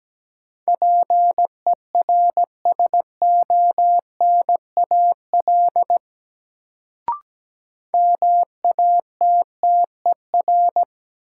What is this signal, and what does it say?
Morse code, heard as a keyed tone, about 17 wpm, 700 Hz. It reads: PERSONAL MATTER
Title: Sentences from Top 700 Words - New Word Drill - 17wpm